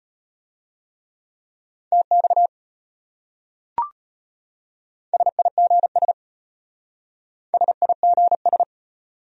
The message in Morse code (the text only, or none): TX SIGS SIGS